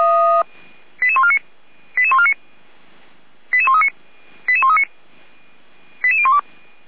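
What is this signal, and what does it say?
Five-tone selective call (ZVEI), heard as quick runs of five tones, tones of about 70 ms. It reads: 88149 88149 88139 88139 88012
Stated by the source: ZVEI five-tone selective call received on SDR, fire brigade (Feuerwehr) Orlamünde.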